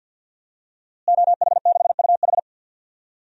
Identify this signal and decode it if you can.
Morse code, heard as a keyed tone, about 51 wpm, 700 Hz. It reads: OH6UH
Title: DXCC-100 Call Signs - 5 Characters - Repeated 2x - 50wpm